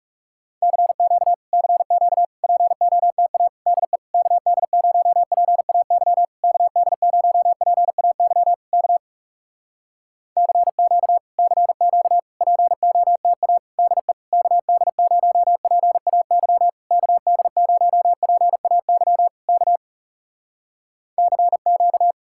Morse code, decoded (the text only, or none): CQ CQ POTA DE KD0PAY KD0PAY K CQ CQ POTA DE KD0PAY KD0PAY K CQ